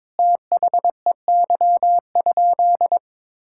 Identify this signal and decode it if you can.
Morse code, heard as a keyed tone, about 21 wpm, 700 Hz. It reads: THEY?
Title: Sentences from Top 1000 Words - Review Part 5 - 22wpm